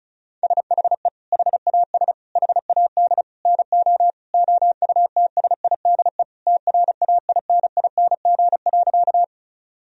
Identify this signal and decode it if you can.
Morse code, heard as a keyed tone, about 36 wpm, 700 Hz. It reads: SHE HAS HAD NO OUTSIDE TRAINING.
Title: Sentences from Top 1000 Words - Review Part 4 - 35wpm